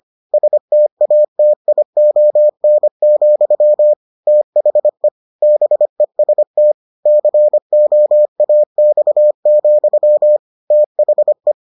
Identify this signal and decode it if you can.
Morse code, heard as a keyed tone, about 25 wpm, 600 Hz. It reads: STATION, THE BEST COAX, THE